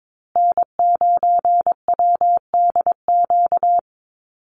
Morse code, heard as a keyed tone, about 22 wpm, 700 Hz. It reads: N9WDQ
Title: USA Call Signs Repeated 2x 22wpm